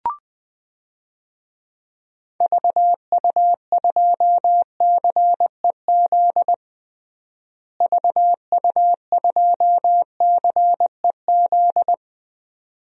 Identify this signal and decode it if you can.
Morse code, heard as a keyed tone, about 20 wpm, 700 Hz. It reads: VU2CEZ VU2CEZ
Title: DXCC-100 Call Signs - 6 Characters - Repeated 2x - 20wpm